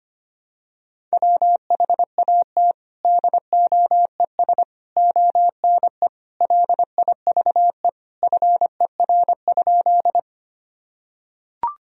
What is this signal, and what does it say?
Morse code, heard as a keyed tone, about 25 wpm, 700 Hz. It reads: WHAT DOES ONE LIVE FER?
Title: Sentences from Top 200 Words and Abbreviations 25wpm